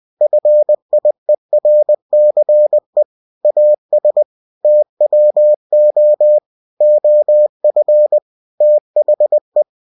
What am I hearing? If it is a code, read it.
Morse code, heard as a keyed tone, about 20 wpm, 600 Hz. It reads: FIERCE AS TWO OF THE
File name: cw-books / winnie-the-pooh / winnie-20wpm-page-0062